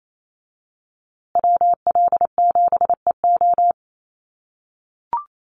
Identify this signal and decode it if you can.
Morse code, heard as a keyed tone, about 28 wpm, 700 Hz. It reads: WL7EO
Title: USA Call Signs Repeated 3x 28wpm